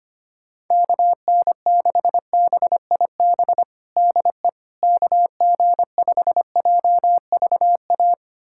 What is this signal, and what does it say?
Morse code, heard as a keyed tone, about 25 wpm, 700 Hz. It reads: KN6BIB DE KG5JVA
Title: USA Callsign DE Callsign - 6 Characters - 25wpm